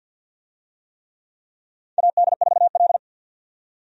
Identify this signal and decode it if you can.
Morse code, heard as a keyed tone, about 47 wpm, 700 Hz. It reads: AD4L
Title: USA Call Signs - 4 Characters - 50wpm